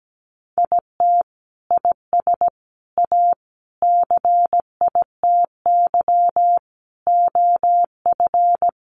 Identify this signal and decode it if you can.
Morse code, heard as a keyed tone, about 17 wpm, 700 Hz. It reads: IT IS A CITY OF
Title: Sentences from Top 1500 Words 17wpm